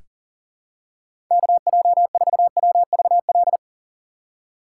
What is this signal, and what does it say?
Morse code, heard as a keyed tone, about 37 wpm, 700 Hz. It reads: KJ4WVL
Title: USA Call Signs - 2x Character Spacing - 40wpm